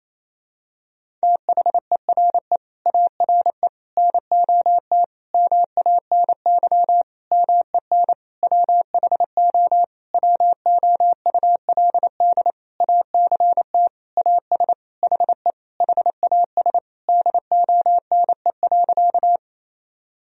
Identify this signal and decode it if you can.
Morse code, heard as a keyed tone, about 28 wpm, 700 Hz. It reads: THERE ARE NOT MANY MEN WHO WOULD ACT AS HE HAS DONE.
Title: Sentences from Top 1000 Words - Review Part 1 - 28wpm